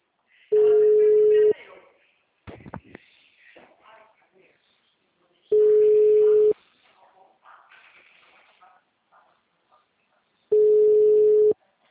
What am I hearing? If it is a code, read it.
Ringback tone (Europe, 425 Hz), with speech.